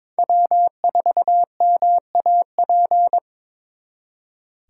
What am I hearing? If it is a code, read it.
Morse code, heard as a keyed tone, about 26 wpm, 700 Hz. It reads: W4MAP
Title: Speed Racing - USA 1x3 Call Signs 22wpm